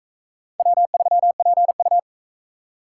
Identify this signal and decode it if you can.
Morse code, heard as a keyed tone, about 38 wpm, 700 Hz. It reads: W3PU